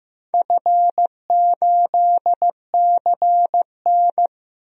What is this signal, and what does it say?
Morse code, heard as a keyed tone, about 16 wpm, 700 Hz. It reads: F8CN
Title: DXCC-100 Call Signs - 4 Characters - Repeated 3x - 15wpm